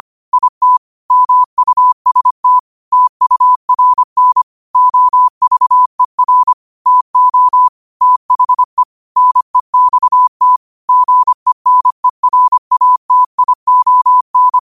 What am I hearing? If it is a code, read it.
Morse code, heard as a keyed tone, about 25 wpm, 1000 Hz. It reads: IT MUST TURN OVER TO THE NEXT GENERATION